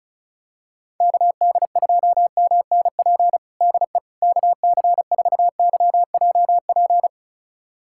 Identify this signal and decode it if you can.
Morse code, heard as a keyed tone, about 36 wpm, 700 Hz. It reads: KD2MNP DE KC4YJP
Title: USA Callsign DE Callsign 35wpm